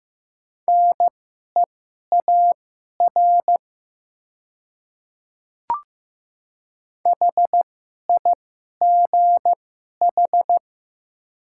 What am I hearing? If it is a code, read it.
Morse code, heard as a keyed tone, about 15 wpm, spaced as for 11 wpm, 700 Hz. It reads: NEAR HIGH
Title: 2x Character Spacing - Top 300 Words - 15wpm